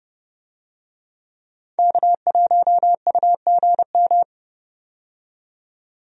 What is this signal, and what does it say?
Morse code, heard as a keyed tone, about 29 wpm, 700 Hz. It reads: K1UGM
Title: USA Call Signs - 5 Characters - 30wpm